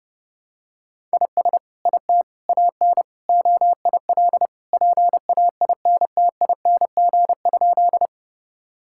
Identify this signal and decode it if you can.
Morse code, heard as a keyed tone, about 32 wpm, 700 Hz. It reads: IS IT AN OIL PAINTING?
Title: Sentences from Top 1000 Words - Review Part 6 - 30wpm